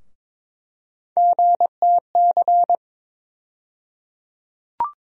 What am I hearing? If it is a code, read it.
Morse code, heard as a keyed tone, about 21 wpm, 700 Hz. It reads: GTC
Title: USA Call Sign Postfixes - 3 Letters 22wpm